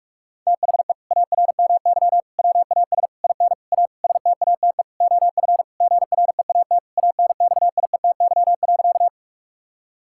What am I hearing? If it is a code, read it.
Morse code, heard as a keyed tone, about 42 wpm, 700 Hz. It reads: THE ARMY WAS IN A STATE OF GREAT ANXIETY.
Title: Sentences from Top 1600 Words 45wpm